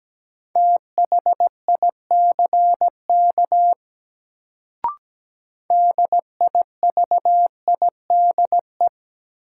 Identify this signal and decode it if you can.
Morse code, heard as a keyed tone, about 17 wpm, 700 Hz. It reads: THICK DIVIDE
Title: Sentences from Top 1600 Words - New Word Drill - 17wpm